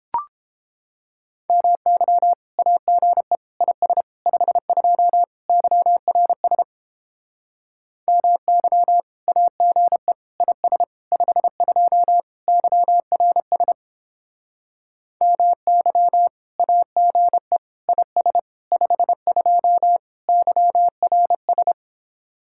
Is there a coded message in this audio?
Yes: Morse code from 1.5 to 21.7 s, heard as a keyed tone, about 29 wpm, 700 Hz. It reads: MY AGE IS 52 YRS MY AGE IS 52 YRS MY AGE IS 52 YRS